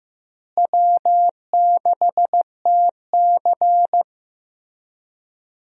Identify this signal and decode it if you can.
Morse code, heard as a keyed tone, about 15 wpm, 700 Hz. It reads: W6TC
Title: USA Call Signs - 4 Characters - Repeated 2x - 15wpm